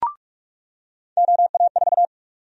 Morse code, heard as a keyed tone, about 46 wpm, 700 Hz. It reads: OA4
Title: USA Call Sign Prefixes - 45wpm